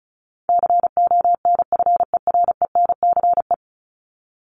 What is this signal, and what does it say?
Morse code, heard as a keyed tone, about 34 wpm, 700 Hz. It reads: CONFERENCE
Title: Top 1751-2000 Words 35wpm